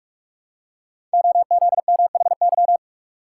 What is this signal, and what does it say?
Morse code, heard as a keyed tone, about 44 wpm, 700 Hz. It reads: OZMHY
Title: Code Groups - 5 Letters - 45wpm